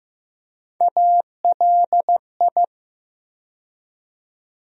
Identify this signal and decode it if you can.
Morse code, heard as a keyed tone, about 15 wpm, 700 Hz. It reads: ALI
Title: USA Call Sign Postfixes - 3 Letters 15wpm